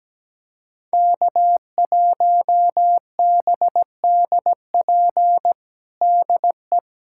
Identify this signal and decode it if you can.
Morse code, heard as a keyed tone, about 17 wpm, 700 Hz. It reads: K1BDP DE
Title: USA Callsign DE Callsign - 5 Characters - 17wpm